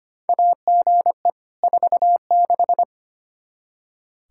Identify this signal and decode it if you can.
Morse code, heard as a keyed tone, about 25 wpm, 700 Hz. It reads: AGE 46
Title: Practice QSO - Exchange 3 - More Info - 25wpm